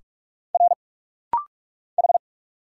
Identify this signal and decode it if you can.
Morse code, heard as a keyed tone, about 44 wpm, 690 Hz. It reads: R H
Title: Single Letter-Number - Rapid-Fire 45wpm